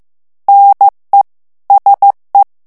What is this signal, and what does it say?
Morse code, heard as a keyed tone, about 15 wpm, 800 Hz. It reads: NE SE
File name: FRANCAIS TEXTE CLAIR 2 VITESSE 14 MOTS MINUTES